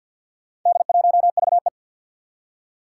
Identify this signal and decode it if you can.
Morse code, heard as a keyed tone, about 50 wpm, 700 Hz. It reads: D1VE
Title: Four Challenging Letter-Number Combinations 50wpm